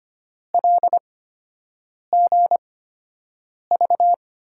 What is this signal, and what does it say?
Morse code, heard as a keyed tone, about 25 wpm, spaced as for 7 wpm, 700 Hz. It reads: LGV
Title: USA Call Signs - 6 Characters - 8x Character Spacing - 25wpm